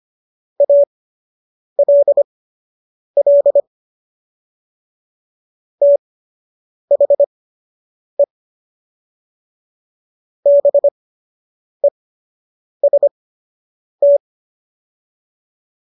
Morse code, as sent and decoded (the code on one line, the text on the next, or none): .- .-.. .-.. / - .... / -... ... -
ALL THE BEST